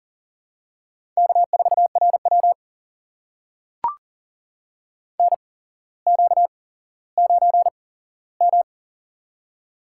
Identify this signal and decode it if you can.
Morse code, heard as a keyed tone, about 42 wpm, spaced as for 17 wpm, 700 Hz. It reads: K4RW NQ9M
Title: USA Call Signs - 4 Characters - 8x Character Spacing - 40wpm